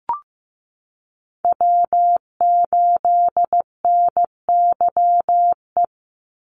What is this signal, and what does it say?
Morse code, heard as a keyed tone, about 15 wpm, 700 Hz. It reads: W8NYE
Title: USA Call Signs 15wpm